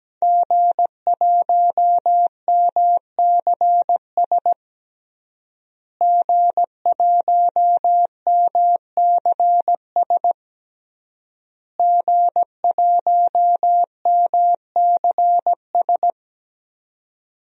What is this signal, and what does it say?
Morse code, heard as a keyed tone, about 17 wpm, 700 Hz. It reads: G1MCS G1MCS G1MCS
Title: DXCC-100 Call Signs - 5 Characters - Repeated 3x - 17wpm